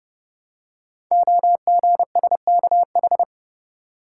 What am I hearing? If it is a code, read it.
Morse code, heard as a keyed tone, about 30 wpm, 700 Hz. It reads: OGSKH
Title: Code Groups - 5 Letters - 30wpm